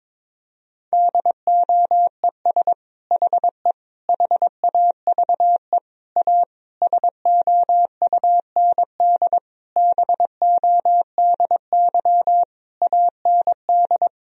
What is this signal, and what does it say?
Morse code, heard as a keyed tone, about 22 wpm, 700 Hz. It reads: DOES HE HAVE A SOUND BODY AND
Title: Sentences from Top 1000 Words - Review Part 2 - 22wpm